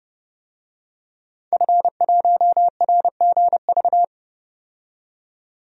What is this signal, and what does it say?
Morse code, heard as a keyed tone, about 29 wpm, 700 Hz. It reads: F1RGV
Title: DXCC-100 Call Signs - 5 Characters - 30wpm